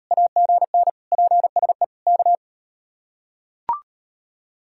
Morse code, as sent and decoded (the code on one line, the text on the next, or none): .- --. -. / .--. ... / -.-
AGN PSE K